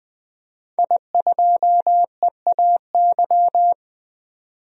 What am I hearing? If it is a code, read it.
Morse code, heard as a keyed tone, about 20 wpm, 700 Hz. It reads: I2EAY
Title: DXCC-100 Call Signs - 5 Characters - 20wpm